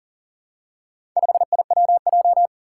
Morse code, heard as a keyed tone, about 40 wpm, 700 Hz. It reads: 5IWJ
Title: Four Challenging Letter-Number Combinations 40wpm